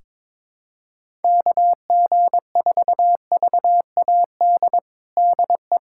Morse code, heard as a keyed tone, about 22 wpm, 700 Hz. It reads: KG4VAD DE
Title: USA Callsign DE Callsign 22wpm